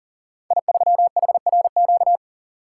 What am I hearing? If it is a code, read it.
Morse code, heard as a keyed tone, about 41 wpm, 700 Hz. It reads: I3HRQ
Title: Code Groups - 5 Letter:Numbers - 40wpm